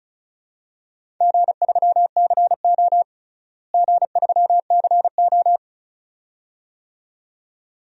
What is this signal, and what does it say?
Morse code, heard as a keyed tone, about 34 wpm, 700 Hz. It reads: G3CO G3CO
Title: DXCC-100 Call Signs - 4 Characters - Repeated 2x - 35wpm